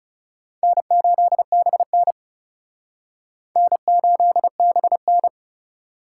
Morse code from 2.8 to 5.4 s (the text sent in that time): N8BN